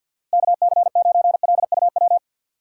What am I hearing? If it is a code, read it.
Morse code, heard as a keyed tone, about 47 wpm, 700 Hz. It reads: KC9LUW
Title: USA Call Signs - 6 Characters - 3x Character Spacing - 50wpm